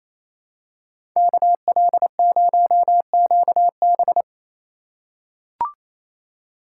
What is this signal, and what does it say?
Morse code, heard as a keyed tone, about 28 wpm, 700 Hz. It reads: KL0QB
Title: USA Call Signs - 2x Character Spacing - 28wpm